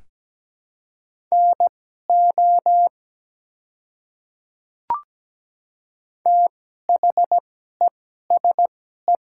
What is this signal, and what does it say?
Morse code, heard as a keyed tone, about 17 wpm, spaced as for 12 wpm, 700 Hz. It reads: NO THESE
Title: 2x Character Spacing - Top 100 Words - 17wpm